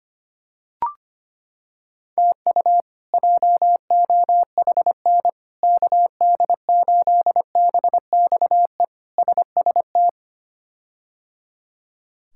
Morse code, heard as a keyed tone, about 25 wpm, 700 Hz. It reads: TU JOHN KD8BXE SST
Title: K1USN SST QSO Practice - 25wpm